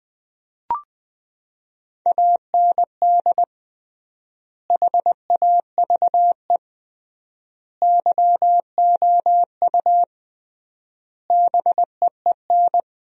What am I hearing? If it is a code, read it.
Morse code, heard as a keyed tone, about 20 wpm, 700 Hz. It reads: AND HAVE YOU BEEN
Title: 3x Word-Spacing - Sentences from Top 200 Words - 20wpm